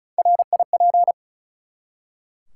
Morse code, heard as a keyed tone, about 36 wpm, 700 Hz. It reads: RIP
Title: Three Letter Words - 35wpm